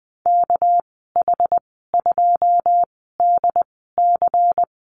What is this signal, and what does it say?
Morse code, heard as a keyed tone, about 20 wpm, spaced as for 15 wpm, 700 Hz. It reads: KH2DC
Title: 207 Lesson 33g - Letter j - Call Signs 20wpm